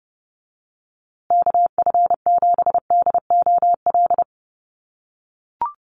Morse code, heard as a keyed tone, about 30 wpm, 700 Hz. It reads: KF7DOL
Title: USA Call Signs Repeated 2x 30wpm